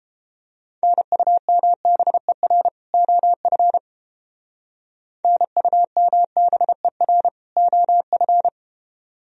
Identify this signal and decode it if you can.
Morse code, heard as a keyed tone, about 30 wpm, 700 Hz. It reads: NUMBER OF NUMBER OF